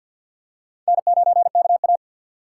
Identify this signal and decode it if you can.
Morse code, heard as a keyed tone, about 53 wpm, 700 Hz. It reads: N9KA
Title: USA Call Signs - 4 Characters - 6x Character Spacing - 50wpm